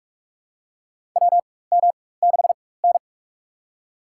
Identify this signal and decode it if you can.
Morse code, heard as a keyed tone, about 46 wpm, spaced as for 21 wpm, 700 Hz. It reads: WM6N
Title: USA Call Signs - 4 Characters - 4x Character Spacing - 45wpm